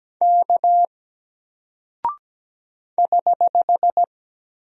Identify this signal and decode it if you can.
Morse code, heard as a keyed tone, about 17 wpm, 700 Hz. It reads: K <HH>
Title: Single Character (Letter:Number:Punctuation:Prosign) Alt 17wpm